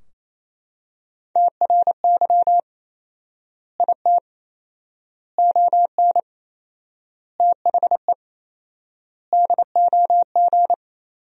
Morse code, heard as a keyed tone, about 28 wpm, 700 Hz. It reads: TRY IT ON THE DOG
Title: 4x Word-Spacing - Sentences from Top 400 Words - 28wpm